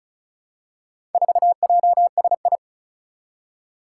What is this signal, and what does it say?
Morse code, heard as a keyed tone, about 36 wpm, 700 Hz. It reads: 4JSI
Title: Four Challenging Letter-Number Combinations 35wpm